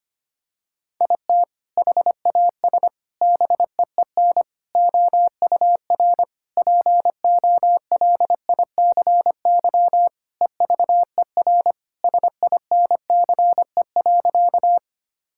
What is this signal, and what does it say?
Morse code, heard as a keyed tone, about 25 wpm, 700 Hz. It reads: IT HAS BEEN OUR POLICY EVER SINCE.
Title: Sentences from Top 1000 Words - Review Part 5 - 25wpm